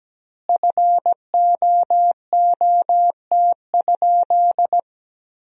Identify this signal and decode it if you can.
Morse code, heard as a keyed tone, about 17 wpm, 700 Hz. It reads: FOOT?
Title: 2x Word-Spacing - Sentences from Top 500 Words - 17wpm